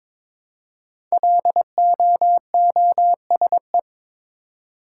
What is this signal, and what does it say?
Morse code, heard as a keyed tone, about 22 wpm, 700 Hz. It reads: LOOSE
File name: 5 Letter Words - First 3 Letters Given - 22wpm